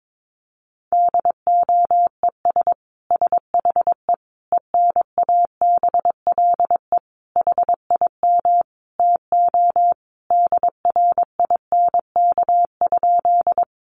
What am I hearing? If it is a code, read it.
Morse code, heard as a keyed tone, about 22 wpm, 700 Hz. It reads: DOES SHE ENABLE HIM TO DRINK?
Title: Sentences from Top 1500 Words 22wpm